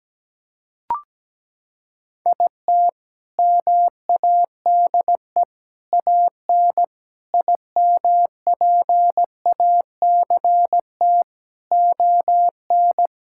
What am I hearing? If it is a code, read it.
Morse code, heard as a keyed tone, about 17 wpm, 700 Hz. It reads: IT MADE AN IMPACT ON
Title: Sentences from Top 1000 Words - Review Part 6 - 17wpm